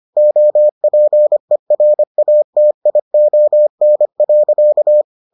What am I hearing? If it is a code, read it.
Morse code, heard as a keyed tone, about 25 wpm, 600 Hz. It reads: OPERATION.